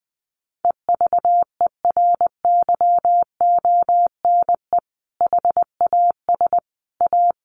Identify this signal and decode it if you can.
Morse code, heard as a keyed tone, about 20 wpm, 700 Hz. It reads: EVERYONE HAS A
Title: Sentences from Top 1200 Words 20wpm